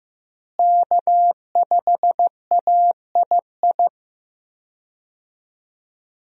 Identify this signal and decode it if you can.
Morse code, heard as a keyed tone, about 15 wpm, 700 Hz. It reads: K5AII
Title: USA Call Signs - 5 Characters - Repeated 2x - 15wpm